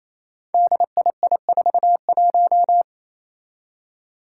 Morse code, heard as a keyed tone, about 29 wpm, 700 Hz. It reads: DII41
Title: Five Challenging Letter-Number Combinations 28wpm